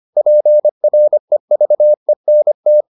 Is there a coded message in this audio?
Yes: Morse code from 0.2 to 2.8 s, heard as a keyed tone, about 25 wpm, 600 Hz. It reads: PREVENT